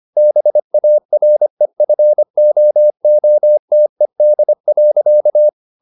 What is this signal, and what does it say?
Morse code, heard as a keyed tone, about 25 wpm, 600 Hz. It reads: BAREFOOTED.
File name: the old man and the sea part 1 25wpm